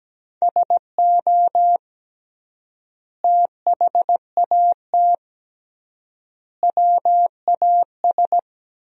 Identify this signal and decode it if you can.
Morse code, heard as a keyed tone, about 17 wpm, 700 Hz. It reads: SO THAT WAS
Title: 3x Word-Spacing - Sentences from Top 100 Words - 17wpm